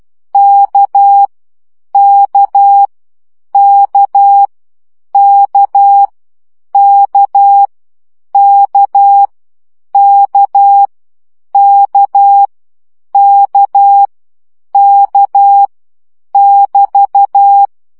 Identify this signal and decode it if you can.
Morse code, heard as a keyed tone, about 12 wpm, 800 Hz. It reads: K K K K K K K K K K <BT>
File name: CW gyakorlas k 10 baud